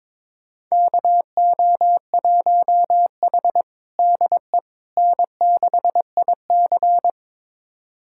Morse code, heard as a keyed tone, about 22 wpm, 700 Hz. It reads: KO1H DE N6IC
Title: USA Callsign DE Callsign - 4 Characters - 22wpm